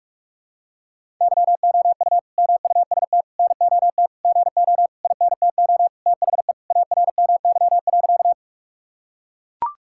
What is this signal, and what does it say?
Morse code, heard as a keyed tone, about 44 wpm, 700 Hz. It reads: YOU MUST NOT GO INTO THE ARMY.
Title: Sentences from Top 1000 Words - Review Part 5 - 45wpm